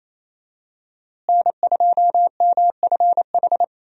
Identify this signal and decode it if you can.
Morse code, heard as a keyed tone, about 27 wpm, 700 Hz. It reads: N2MFH
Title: USA Call Signs - 5 Characters - Repeated 2x - 28wpm